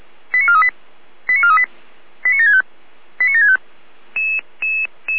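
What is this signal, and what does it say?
Five-tone selective call (ZVEI), heard as quick runs of five tones, tones of about 80 ms. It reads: 89438 89438 89765 89765
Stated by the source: Five-tone ZVEI selective call received by SDR.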